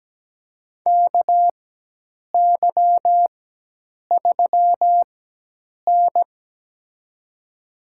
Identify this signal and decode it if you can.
Morse code, heard as a keyed tone, about 17 wpm, spaced as for 8 wpm, 700 Hz. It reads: KY3N
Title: USA Call Signs - 4 Characters - 4x Character Spacing - 17wpm